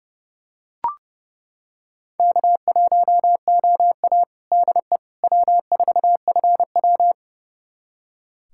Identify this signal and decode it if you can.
Morse code, heard as a keyed tone, about 29 wpm, 700 Hz. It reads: K1OA DE W4FW
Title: USA Callsign DE Callsign - 4 Characters - 30wpm